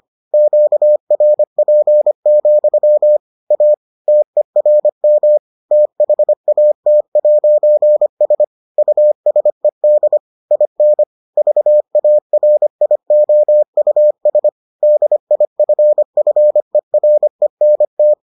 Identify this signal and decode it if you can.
Morse code, heard as a keyed tone, about 25 wpm, 600 Hz. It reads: QRP, A TERM THAT'S USED IN VARIOUS DIFFERENT